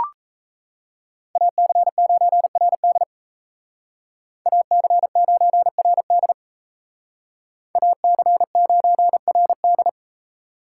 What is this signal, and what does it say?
Morse code, heard as a keyed tone, about 36 wpm, 700 Hz. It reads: AC9RD AC9RD AC9RD